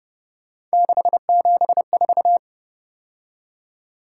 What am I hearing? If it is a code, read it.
Morse code, heard as a keyed tone, about 30 wpm, 700 Hz. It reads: 674